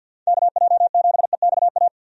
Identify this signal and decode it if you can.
Morse code, heard as a keyed tone, about 51 wpm, 700 Hz. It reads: KJ7EXA
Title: USA Call Signs - 6 Characters - Repeated 2x - 50wpm